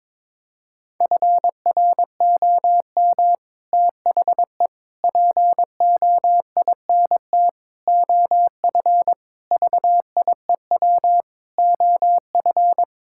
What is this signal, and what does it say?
Morse code, heard as a keyed tone, about 22 wpm, 700 Hz. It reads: FROM THE POINT OF VIEW OF
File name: practice / 22wpm / Sets of 6 Words - 22wpm